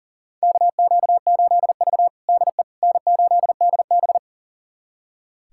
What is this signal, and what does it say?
Morse code, heard as a keyed tone, about 40 wpm, 700 Hz. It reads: KQ8V DE N8DB
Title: USA Callsign DE Callsign - 4 Characters - 40wpm